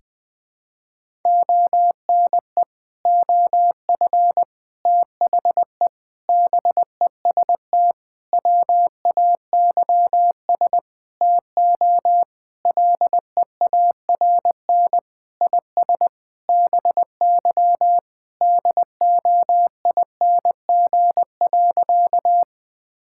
Morse code, heard as a keyed tone, about 21 wpm, 700 Hz. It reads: ONE OF THE BEST WAYS TO LEARN IS BY DOING.
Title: Sentences with 6-Word Phrases - 20wpm